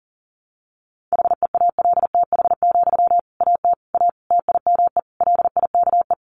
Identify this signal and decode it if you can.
Morse code, heard as a keyed tone, about 40 wpm, 700 Hz. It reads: HEALTH, AT A TIME LIKE
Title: Sentences from Top 600 Words 40wpm